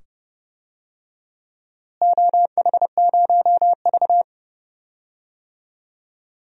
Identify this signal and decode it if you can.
Morse code, heard as a keyed tone, about 30 wpm, 700 Hz. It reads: OH0V
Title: DXCC-100 Call Signs - 4 Characters - Repeated 2x - 30wpm